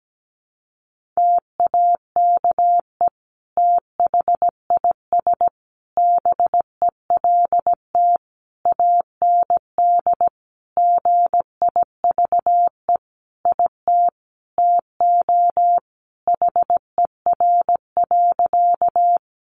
Morse code, heard as a keyed tone, about 17 wpm, 700 Hz. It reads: TAKE THIS BELT AND GIVE IT TO HER.
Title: Sentences from Top 1400 Words 17wpm